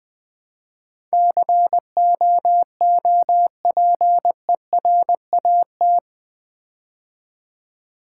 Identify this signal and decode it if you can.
Morse code, heard as a keyed tone, about 20 wpm, 700 Hz. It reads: COOPERAT
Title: Build A Word - 4 Letters At A Time - 11-Letter Words - 20wpm